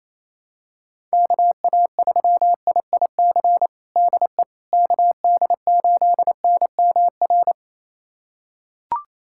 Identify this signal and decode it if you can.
Morse code, heard as a keyed tone, about 28 wpm, 700 Hz. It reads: KA3IIC DE KD8NMR
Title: USA Callsign DE Callsign - 6 Characters - 28wpm